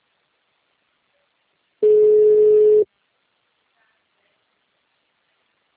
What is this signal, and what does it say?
Ringback tone (Europe, 425 Hz).
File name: out- -2031- - - .1750